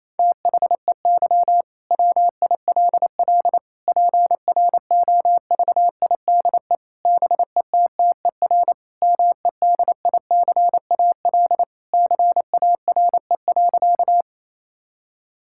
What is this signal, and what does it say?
Morse code, heard as a keyed tone, about 28 wpm, 700 Hz. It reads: THEY WILL PROVIDE BETTER MEDICAL CARE.
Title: Sentences from Top 1000 Words - Review Part 5 - 28wpm